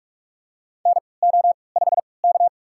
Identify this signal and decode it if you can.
Morse code, heard as a keyed tone, about 43 wpm, spaced as for 25 wpm, 700 Hz. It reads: NO5K
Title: USA Call Signs - 4 Characters - 3x Character Spacing - 45wpm